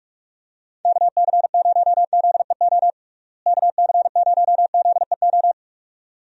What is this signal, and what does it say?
Morse code, heard as a keyed tone, about 46 wpm, 700 Hz. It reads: KC0ZEO KC0ZEO
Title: USA Call Signs - 6 Characters - Repeated 2x - 45wpm